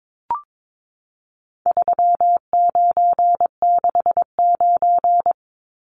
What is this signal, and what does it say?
Morse code, heard as a keyed tone, about 22 wpm, 700 Hz. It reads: 3969
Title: Numbers - 4-Digit 22wpm